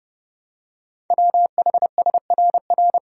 Morse code, heard as a keyed tone, about 30 wpm, 700 Hz. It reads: WHSRR
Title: Code Groups - 5 Letters - 30wpm